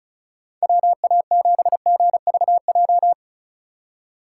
Morse code, heard as a keyed tone, about 36 wpm, 700 Hz. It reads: WA7GVJ
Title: USA Call Signs - 2x Character Spacing - 35wpm